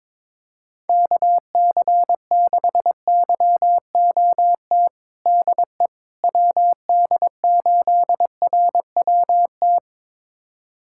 Morse code, heard as a keyed tone, about 22 wpm, 700 Hz. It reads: KC6YOT DE WD8RWT
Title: USA Callsign DE Callsign 22wpm